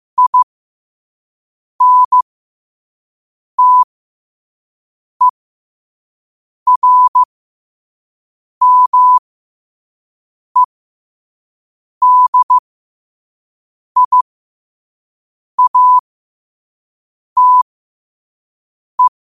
Morse code, heard as a keyed tone, about 15 wpm, spaced as for 5 wpm, 1000 Hz. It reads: INTERMEDIATE
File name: qotdfiles / 5wpm